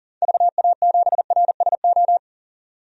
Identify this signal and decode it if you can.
Morse code, heard as a keyed tone, about 37 wpm, 700 Hz. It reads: VA7RSO
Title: DXCC-100 Call Signs - 6 Characters - 40wpm